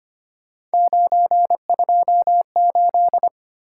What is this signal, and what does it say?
Morse code, heard as a keyed tone, about 25 wpm, 700 Hz. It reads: 928
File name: Three Letter-Number Combinations - 25wpm